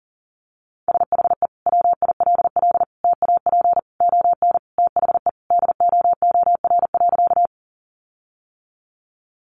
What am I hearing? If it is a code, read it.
Morse code, heard as a keyed tone, about 40 wpm, 700 Hz. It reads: SHE WILL TAP ON THE DOOR.
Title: Sentences from Top 1400 Words 40wpm